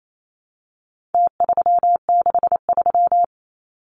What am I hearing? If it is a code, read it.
Morse code, heard as a keyed tone, about 28 wpm, 700 Hz. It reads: T363
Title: Numbers - Contest Style - 4-Digit 28wpm